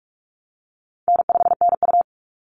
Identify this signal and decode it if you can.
Morse code, heard as a keyed tone, about 44 wpm, 700 Hz. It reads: N5NU
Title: Easy Call Signs Repeated 2x - 45wpm